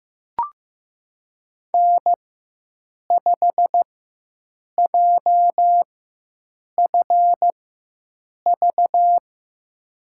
Morse code, heard as a keyed tone, about 15 wpm, spaced as for 7 wpm, 700 Hz. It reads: N5JFV
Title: USA Call Signs - 4x Character Spacing - 15wpm